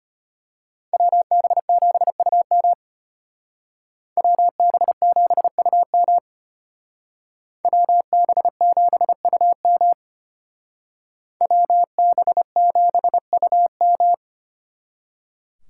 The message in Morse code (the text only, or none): WB7UM WB7UM WB7UM WB7UM